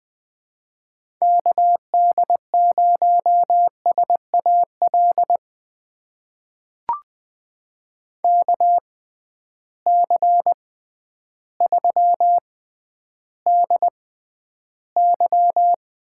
Morse code, heard as a keyed tone, about 20 wpm, spaced as for 11 wpm, 700 Hz. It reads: KD0SAL KC3DY
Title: USA Call Signs - 6 Characters - 6x Character Spacing - 20wpm